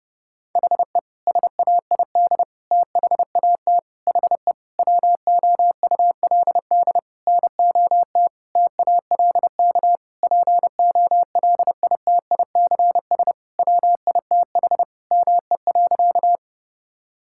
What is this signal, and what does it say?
Morse code, heard as a keyed tone, about 30 wpm, 700 Hz. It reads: HE SAID THAT HE WOULD NOT TALK POLITICS WITH ME.
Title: Sentences from Top 1000 Words - Review Part 6 - 30wpm